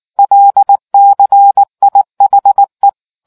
Morse code, heard as a keyed tone, about 19 wpm, 800 Hz. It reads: LCIHE